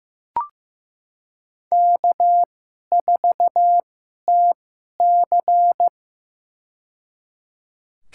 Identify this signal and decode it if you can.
Morse code, heard as a keyed tone, about 15 wpm, spaced as for 11 wpm, 700 Hz. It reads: K4TC
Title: USA Call Signs - 4 Characters - 2x Character Spacing - 15wpm